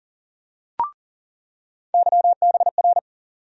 Morse code, heard as a keyed tone, about 40 wpm, 700 Hz. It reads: YBR
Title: USA Call Sign Postfixes - 3 Letters 40wpm